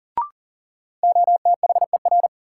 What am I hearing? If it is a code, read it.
Morse code, heard as a keyed tone, about 37 wpm, 700 Hz. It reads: OTHER
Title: Sentences from Top 100 Words - New Word Drill - 40wpm